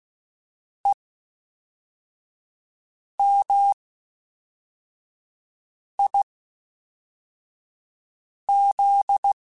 Morse code, heard as a keyed tone, about 16 wpm, 800 Hz. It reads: E M I Z
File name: Special INGs / Special INGs-mp3s / Fast ing @10